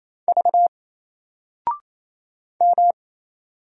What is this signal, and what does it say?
Morse code, heard as a keyed tone, about 28 wpm, 700 Hz. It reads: V M
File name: Single-Letter-Number-28wpm